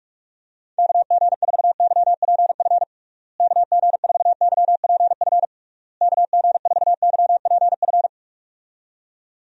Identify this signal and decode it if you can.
Morse code, heard as a keyed tone, about 46 wpm, 700 Hz. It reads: KG4YPF KG4YPF KG4YPF
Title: USA Call Signs - 6 Characters - Repeated 3x - 45wpm